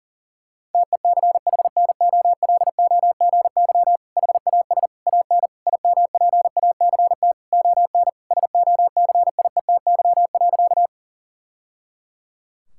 Morse code, heard as a keyed tone, about 40 wpm, 700 Hz. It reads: TECHNOLOGY HAS AN IMPACT ON SOCIETY.
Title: Sentences from Top 1000 Words - Review Part 6 - 40wpm